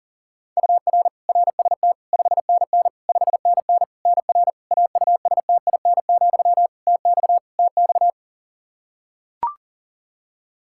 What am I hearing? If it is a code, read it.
Morse code, heard as a keyed tone, about 37 wpm, 700 Hz. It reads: UR RST 5NN 5NN NR AUSTIN, TX TX
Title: Practice QSO - Exchange 1 - Beginning - 40wpm